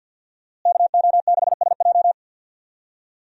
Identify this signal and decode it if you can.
Morse code, heard as a keyed tone, about 51 wpm, 700 Hz. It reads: KO6SJ
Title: USA Call Signs - 5 Characters - Repeated 2x - 50wpm